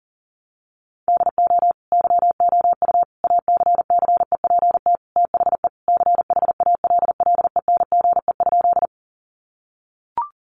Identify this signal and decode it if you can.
Morse code, heard as a keyed tone, about 40 wpm, 700 Hz. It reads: DO YOU ACCEPT THE CHALLENGE?
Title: Sentences from Top 1300 Words 40wpm